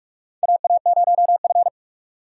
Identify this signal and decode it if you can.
Morse code, heard as a keyed tone, about 46 wpm, 700 Hz. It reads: AA0F
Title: USA Call Signs - 4 Characters - 2x Character Spacing - 45wpm